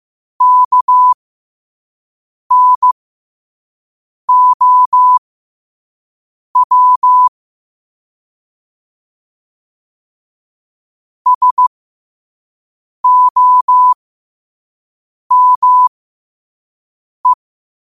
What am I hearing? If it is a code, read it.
Morse code, heard as a keyed tone, about 15 wpm, spaced as for 5 wpm, 1000 Hz. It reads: KNOW SOME